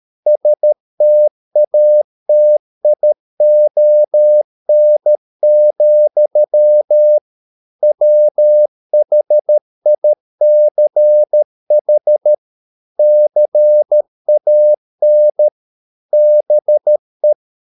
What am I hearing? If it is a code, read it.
Morse code, heard as a keyed tone, about 13 wpm, 600 Hz. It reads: STATION, WHICH CAN BE